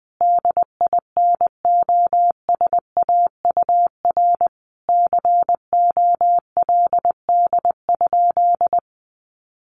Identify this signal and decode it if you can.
Morse code, heard as a keyed tone, about 20 wpm, 700 Hz. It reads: DINOSAUR COLD?
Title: A Sound Of Thunder 20wpm